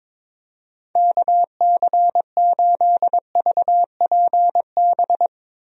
Morse code, heard as a keyed tone, about 22 wpm, 700 Hz. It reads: KC8VPB
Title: USA Call Signs - 6 Characters - Repeated 2x - 22wpm